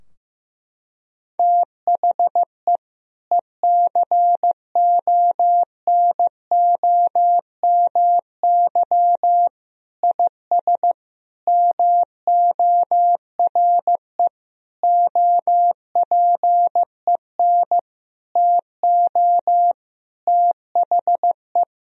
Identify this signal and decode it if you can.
Morse code, heard as a keyed tone, about 15 wpm, 700 Hz. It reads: THE ECONOMY IS MORE OPEN TO THE
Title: Sentences from Top 1000 Words - Review Part 3 - 15wpm